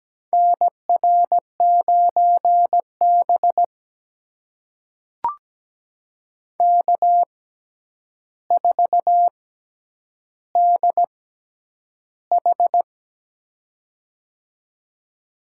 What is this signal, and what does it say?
Morse code, heard as a keyed tone, about 17 wpm, spaced as for 9 wpm, 700 Hz. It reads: NR9B K4DH
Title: USA Call Signs - 4 Characters - 6x Character Spacing - 17wpm